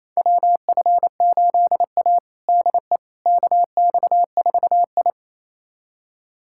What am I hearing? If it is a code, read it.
Morse code, heard as a keyed tone, about 28 wpm, 700 Hz. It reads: WF8A DE KX4I
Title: USA Callsign DE Callsign - 4 Characters - 28wpm